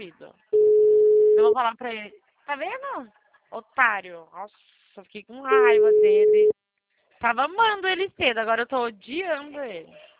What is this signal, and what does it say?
Ringback tone (Europe, 425 Hz), with speech.